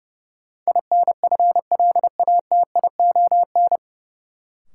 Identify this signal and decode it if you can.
Morse code, heard as a keyed tone, about 30 wpm, 700 Hz. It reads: INFLATION
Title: Sentences from Top 1100 Words - New Word Drill - 30wpm